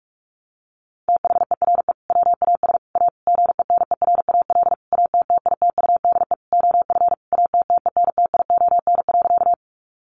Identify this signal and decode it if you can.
Morse code, heard as a keyed tone, about 46 wpm, 700 Hz. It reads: THERE WAS A GENERAL ATTITUDE OF ATTENTION.
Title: Sentences from Top 1300 Words 45wpm